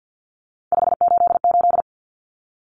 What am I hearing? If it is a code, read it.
Morse code, heard as a keyed tone, about 50 wpm, 700 Hz. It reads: 588
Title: Numbers - 3-Digit 50wpm